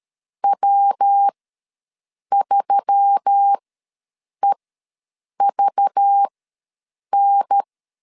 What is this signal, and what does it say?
Morse code, heard as a keyed tone, about 13 wpm, spaced as for 7 wpm, 800 Hz. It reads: W3EVN
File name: LEZIONI / 6 PKX0 / 6 PK ws 12wpm 3116